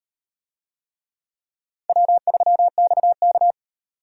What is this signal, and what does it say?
Morse code, heard as a keyed tone, about 38 wpm, 700 Hz. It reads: W3XK